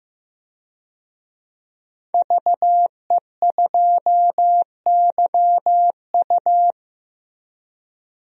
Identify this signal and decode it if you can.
Morse code, heard as a keyed tone, about 15 wpm, 700 Hz. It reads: VE2YU
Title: DXCC-100 Call Signs - 5 Characters - Repeated 2x - 15wpm